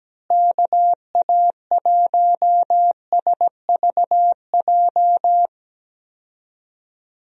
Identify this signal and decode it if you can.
Morse code, heard as a keyed tone, about 17 wpm, 700 Hz. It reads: KA1SVJ
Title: USA Call Signs - 6 Characters - Repeated 2x - 17wpm